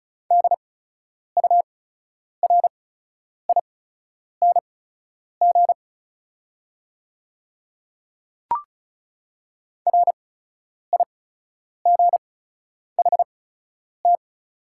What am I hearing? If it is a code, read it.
Morse code, heard as a keyed tone, about 36 wpm, spaced as for 10 wpm, 700 Hz. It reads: DURING RIGHT
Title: 8x Character Spacing - Top 500 Words - 35wpm